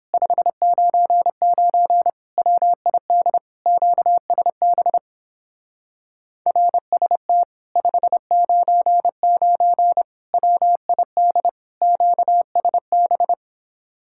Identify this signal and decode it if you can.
Morse code, heard as a keyed tone, about 29 wpm, 700 Hz. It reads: 599 WID QSB RST 599 WID QSB